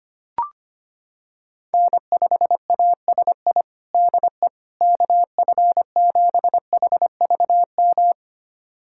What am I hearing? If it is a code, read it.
Morse code, heard as a keyed tone, about 25 wpm, 700 Hz. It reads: N5ASI DE KF7HVM
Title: USA Callsign DE Callsign 25wpm